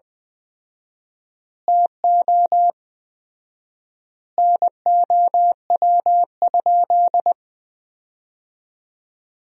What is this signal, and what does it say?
Morse code, heard as a keyed tone, about 21 wpm, 700 Hz. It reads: TO NOW?
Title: 4x Word-Spacing - Sentences from Top 100 Words - 20wpm